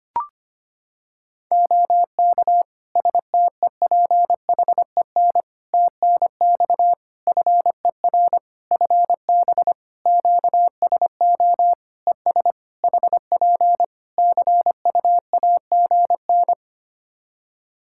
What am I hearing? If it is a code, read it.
Morse code, heard as a keyed tone, about 25 wpm, 700 Hz. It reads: OK STEPHEN TNX FER FB QSO ES HP CUAGN
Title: K1USN SST QSO Practice 25wpm